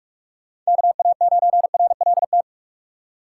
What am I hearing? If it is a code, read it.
Morse code, heard as a keyed tone, about 46 wpm, 700 Hz. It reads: KA9RLT
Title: USA Call Signs - 6 Characters - 4x Character Spacing - 45wpm